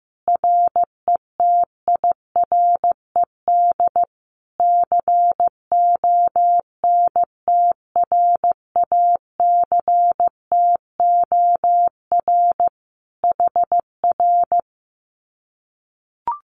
Morse code, heard as a keyed tone, about 15 wpm, 700 Hz. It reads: RETIRED CONTRACTOR HR
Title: Common QSO Phrases Encore 15wpm